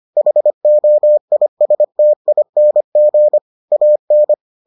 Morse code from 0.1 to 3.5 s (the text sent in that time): HOISTING